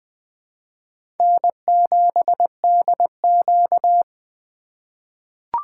Morse code, heard as a keyed tone, about 20 wpm, 700 Hz. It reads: N7DQ
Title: USA Call Signs - 4 Characters - 4x Character Spacing - 20wpm